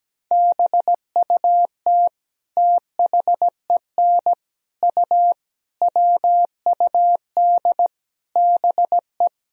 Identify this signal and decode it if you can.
Morse code, heard as a keyed tone, about 17 wpm, 700 Hz. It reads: BUT THEN U WUD BE
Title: Sentences from Top 300 Words and Abbreviations 17wpm